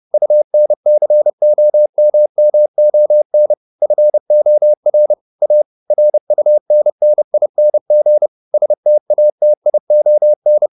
Morse code, heard as a keyed tone, about 30 wpm, 600 Hz. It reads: UNCOMMON FOR A RUNNING STATION